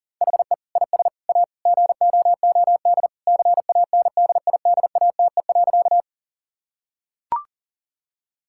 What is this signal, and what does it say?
Morse code, heard as a keyed tone, about 40 wpm, 700 Hz. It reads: HE IS A GOOD CANDIDATE.
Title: Sentences from Top 1600 Words 40wpm